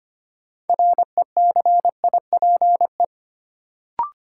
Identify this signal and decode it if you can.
Morse code, heard as a keyed tone, about 25 wpm, 700 Hz. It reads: RECIPE
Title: Sentences from Top 1200 Words - New Word Drill - 25wpm